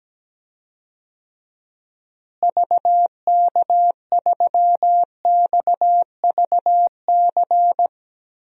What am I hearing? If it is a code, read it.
Morse code, heard as a keyed tone, about 17 wpm, 700 Hz. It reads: VK3XVC
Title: DXCC-100 Call Signs - 6 Characters - Repeated 3x - 17wpm